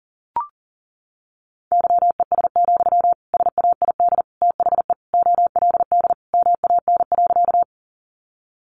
Morse code, heard as a keyed tone, about 40 wpm, 700 Hz. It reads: YES, SAID THE OLD MAN.
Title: Sentences from Top 500 Words 40wpm